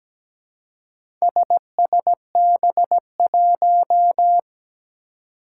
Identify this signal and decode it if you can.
Morse code, heard as a keyed tone, about 17 wpm, 700 Hz. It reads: SSB1
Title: Four Challenging Letter-Number Combinations 17wpm